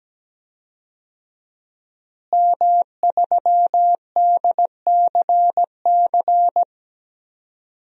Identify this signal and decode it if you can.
Morse code, heard as a keyed tone, about 17 wpm, 700 Hz. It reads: M3DCC
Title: DXCC-100 Call Signs - 5 Characters - Repeated 3x - 17wpm